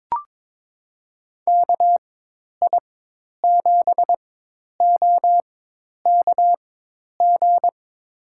Morse code, heard as a keyed tone, about 22 wpm, spaced as for 10 wpm, 700 Hz. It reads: KI7OKG
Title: USA Call Signs - 6 Characters - 4x Character Spacing - 22wpm